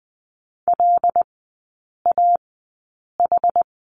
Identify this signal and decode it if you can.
Morse code, heard as a keyed tone, about 19 wpm, spaced as for 8 wpm, 700 Hz. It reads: LAH
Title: 127 Lesson 21e - Number 6 - Three Characters 20wpm